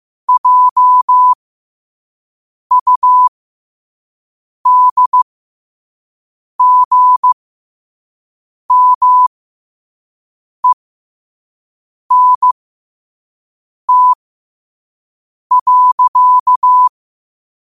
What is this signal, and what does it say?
Morse code, heard as a keyed tone, about 15 wpm, spaced as for 5 wpm, 1000 Hz. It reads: JUDGMENT.
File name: qotdfiles / 5wpm